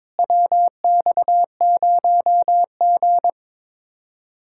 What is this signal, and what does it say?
Morse code, heard as a keyed tone, about 22 wpm, 700 Hz. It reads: WX0G
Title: Speed-Racing Common QSO Phrases Encore - 22wpm